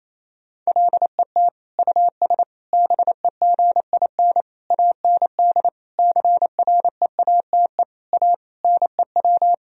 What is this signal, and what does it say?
Morse code, heard as a keyed tone, about 28 wpm, 700 Hz. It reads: LET US BEGIN AND CREATE A NEW
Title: Sentences from Top 1000 Words - Review Part 3 - 28wpm